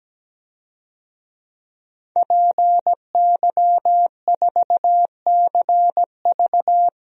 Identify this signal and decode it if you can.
Morse code, heard as a keyed tone, about 17 wpm, 700 Hz. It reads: PY4CV
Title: DXCC-100 Call Signs - 5 Characters - Repeated 2x - 17wpm